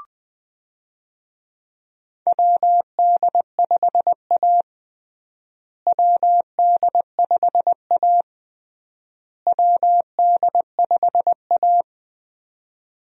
Morse code, heard as a keyed tone, about 20 wpm, 700 Hz. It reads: WD5A WD5A WD5A
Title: USA Call Signs - 4 Characters - Repeated 3x - 20wpm